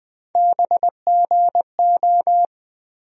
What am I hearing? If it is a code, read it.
Morse code, heard as a keyed tone, about 20 wpm, 700 Hz. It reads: BGO
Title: USA Call Sign Postfixes - 3 Letters 20wpm